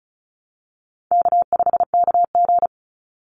Morse code, heard as a keyed tone, about 36 wpm, 700 Hz. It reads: K5KG
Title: Call Signs - 35wpm